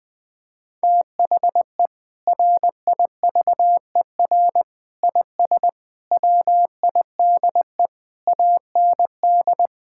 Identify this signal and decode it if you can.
Morse code, heard as a keyed tone, about 20 wpm, 700 Hz. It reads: THE RIVER IS WIDE AND
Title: Sentences from Top 1000 Words - Review Part 6 - 20wpm